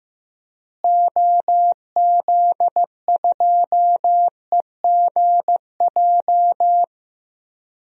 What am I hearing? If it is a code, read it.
Morse code, heard as a keyed tone, about 15 wpm, 700 Hz. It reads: OZ2EGJ
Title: DXCC-100 Call Signs - 6 Characters - Repeated 2x - 15wpm